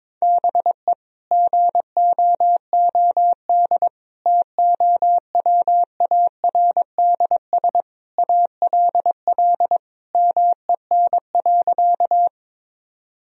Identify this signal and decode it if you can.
Morse code, heard as a keyed tone, about 22 wpm, 700 Hz. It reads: BE GOOD TOWARDS ALL MEN.
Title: Sentences from Top 1000 Words - Review Part 2 - 22wpm